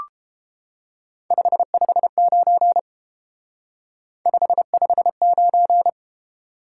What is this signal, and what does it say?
Morse code, heard as a keyed tone, about 30 wpm, 700 Hz. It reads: 559 559